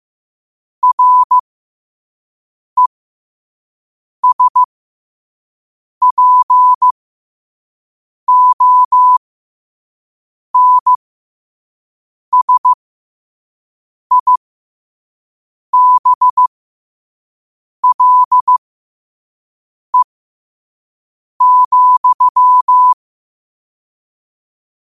Morse code, heard as a keyed tone, about 15 wpm, spaced as for 5 wpm, 1000 Hz. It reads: RESPONSIBLE,
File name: qotdfiles / 5wpm